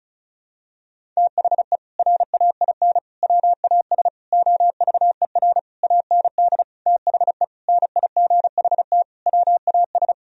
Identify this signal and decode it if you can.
Morse code, heard as a keyed tone, about 36 wpm, 700 Hz. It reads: THE RAIN WAS OVER AND THE NIGHT WAS
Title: Sentences from Top 1000 Words - Review Part 1 - 35wpm